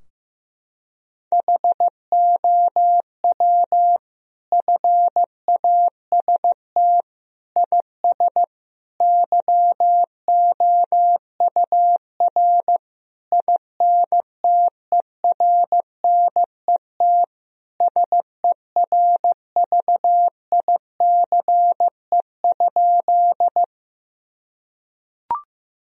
Morse code, heard as a keyed tone, about 16 wpm, 700 Hz. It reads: HOW FAST IS YOUR INTERNET SERVICE?
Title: Sentences from Top 1000 Words - Review Part 4 - 15wpm